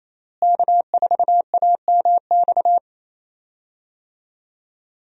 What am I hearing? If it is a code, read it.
Morse code, heard as a keyed tone, about 27 wpm, 700 Hz. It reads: K4AMX
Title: USA Call Signs - 5 Characters - Repeated 2x - 28wpm